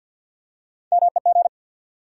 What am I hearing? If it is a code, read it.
Morse code, heard as a keyed tone, about 52 wpm, 700 Hz. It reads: MEG
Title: USA Common Names - Max 4 Letters - 50wpm